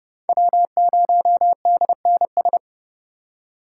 Morse code, heard as a keyed tone, about 32 wpm, 700 Hz. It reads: W0DNS
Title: USA Call Signs - 5 Characters - 3x Character Spacing - 30wpm